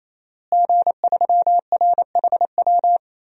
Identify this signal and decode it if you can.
Morse code, heard as a keyed tone, about 27 wpm, 700 Hz. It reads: G3RHW
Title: DXCC-100 Call Signs - 5 Characters - Repeated 3x - 28wpm